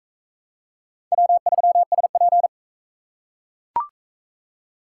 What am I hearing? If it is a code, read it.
Morse code, heard as a keyed tone, about 37 wpm, 700 Hz. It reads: W3SP